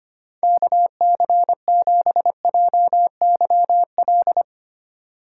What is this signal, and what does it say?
Morse code, heard as a keyed tone, about 31 wpm, 700 Hz. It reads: KC7JYL
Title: Speed Racing - USA Call Signs 25wpm